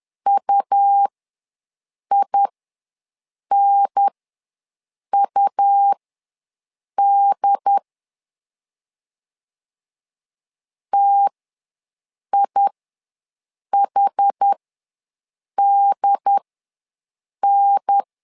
Morse code, heard as a keyed tone, about 10 wpm, spaced as for 6 wpm, 800 Hz. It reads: UINUD TIHDN
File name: 2 UHDN ws 10wpm